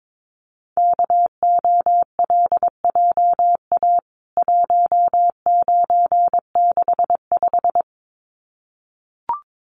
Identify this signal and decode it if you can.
Morse code, heard as a keyed tone, about 22 wpm, 700 Hz. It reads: KOLJA 1965
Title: CWT Exchange 22wpm